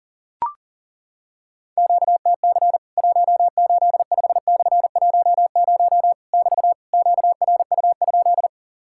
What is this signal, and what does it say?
Morse code, heard as a keyed tone, about 42 wpm, 700 Hz. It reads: QTC 185/10 <BT> QRU?
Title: Worked All Europe DX Contest - QTC Exchange - 40wpm